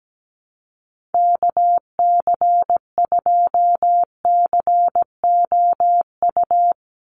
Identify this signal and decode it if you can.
Morse code, heard as a keyed tone, about 17 wpm, 700 Hz. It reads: KC2COU
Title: USA Call Signs 17wpm